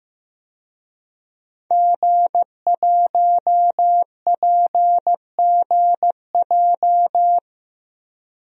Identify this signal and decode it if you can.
Morse code, heard as a keyed tone, about 15 wpm, 700 Hz. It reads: G1PGJ
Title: DXCC-100 Call Signs - 5 Characters - 15wpm